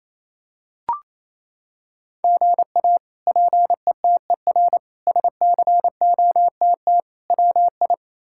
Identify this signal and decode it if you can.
Morse code, heard as a keyed tone, about 27 wpm, 700 Hz. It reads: GA PETER SCOTT WI
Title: K1USN SST QSO Practice - 28wpm